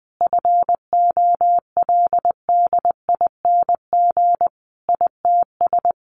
Morse code, heard as a keyed tone, about 20 wpm, 700 Hz. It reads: FOLDING ITS
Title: A Sound Of Thunder 20wpm